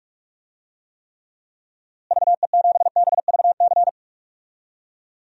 Morse code, heard as a keyed tone, about 41 wpm, 700 Hz. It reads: VE7BVC
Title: DXCC-100 Call Signs - 6 Characters - Repeated 2x - 45wpm